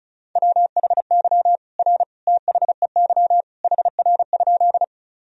Morse code, heard as a keyed tone, about 36 wpm, 700 Hz. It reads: WHY R THEY HR?
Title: Sentences from Top 200 Words and Abbreviations 35wpm